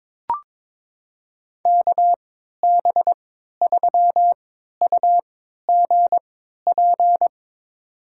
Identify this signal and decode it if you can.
Morse code, heard as a keyed tone, about 22 wpm, spaced as for 13 wpm, 700 Hz. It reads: KB3UGP
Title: USA Call Signs - 6 Characters - 3x Character Spacing - 22wpm